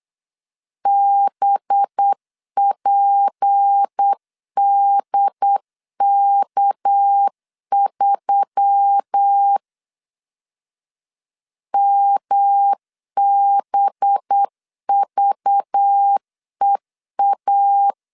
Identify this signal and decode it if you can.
Morse code, heard as a keyed tone, about 8 wpm, 800 Hz. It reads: BPDK3 MBVEA
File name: LEZIONI / 6 PKX0 / 6 PKX 8wpm 3041